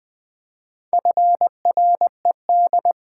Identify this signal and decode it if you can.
Morse code, heard as a keyed tone, about 21 wpm, 700 Hz. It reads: FRED
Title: Pangram Challenge v2 - 20wpm